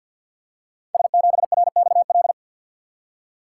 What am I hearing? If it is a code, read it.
Morse code, heard as a keyed tone, about 53 wpm, 700 Hz. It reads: S7RXL
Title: Code Groups - 5 Letter:Numbers - 50wpm